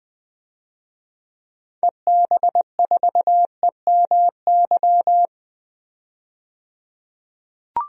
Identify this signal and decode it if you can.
Morse code, heard as a keyed tone, about 20 wpm, 700 Hz. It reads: EB4EMY
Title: DXCC-100 Call Signs - 6 Characters - 20wpm